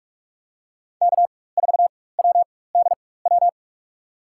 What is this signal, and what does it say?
Morse code, heard as a keyed tone, about 46 wpm, spaced as for 21 wpm, 700 Hz. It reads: K4WDW
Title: USA Call Signs - 4x Character Spacing - 45wpm